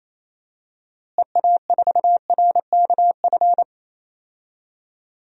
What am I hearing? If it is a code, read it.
Morse code, heard as a keyed tone, about 27 wpm, 700 Hz. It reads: EA4RKF
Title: DXCC-100 Call Signs - 6 Characters - Repeated 3x - 28wpm